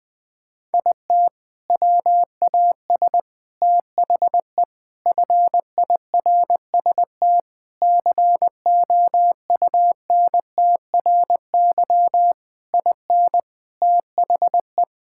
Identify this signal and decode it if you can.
Morse code, heard as a keyed tone, about 19 wpm, 700 Hz. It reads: IT WAS THE FIRST COUNTRY IN THE
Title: Sentences with 7-Word Phrases - Spoken Beginning - 20wpm